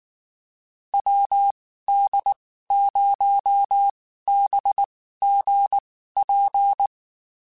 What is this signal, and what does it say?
Morse code, heard as a keyed tone, about 19 wpm, spaced as for 14 wpm, 800 Hz. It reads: WD0BGP
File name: (spaced) / QSOs Spaced-mp3s / WD0BGP 1 @18 sp